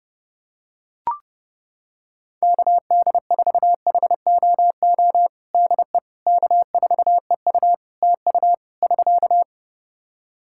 Morse code, heard as a keyed tone, about 30 wpm, 700 Hz. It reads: KD4HOO DE K4EU TU <SK>
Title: Practice QSO - Exchange 4 - Ending - 30wpm